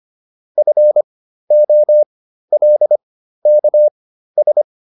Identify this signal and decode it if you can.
Morse code, heard as a keyed tone, about 25 wpm, spaced as for 13 wpm, 600 Hz. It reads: FOLKS